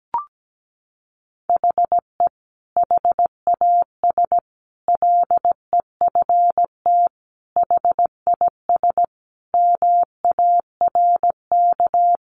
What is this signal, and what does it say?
Morse code, heard as a keyed tone, about 17 wpm, 700 Hz. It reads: HE HAS LEFT HIS MARK
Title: Sentences from Top 1100 Words 17wpm